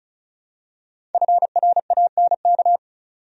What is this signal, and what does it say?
Morse code, heard as a keyed tone, about 36 wpm, 700 Hz. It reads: FRANK
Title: USA Common Names - Max 5 Letters - 35wpm